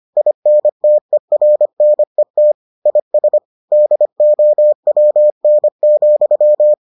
Morse code, heard as a keyed tone, about 25 wpm, 600 Hz. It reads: INTERNET IS DOWN,